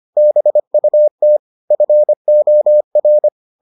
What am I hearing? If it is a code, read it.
Morse code, heard as a keyed tone, about 25 wpm, 600 Hz. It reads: BUT FOR